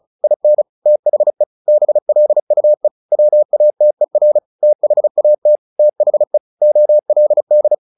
Morse code, heard as a keyed tone, about 34 wpm, 600 Hz. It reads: IN THE BLUE WATER THAT THE OLD
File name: the old man and the sea / the old man and the sea 1 35wpm